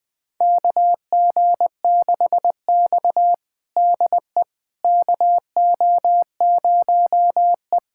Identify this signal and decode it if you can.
Morse code, heard as a keyed tone, about 21 wpm, 700 Hz. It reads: KG6X DE KO0E
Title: USA Callsign DE Callsign - 4 Characters - 20wpm